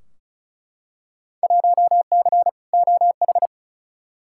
Morse code, heard as a keyed tone, about 36 wpm, 700 Hz. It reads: 1C OH
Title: Field Day Exchanges - 35wpm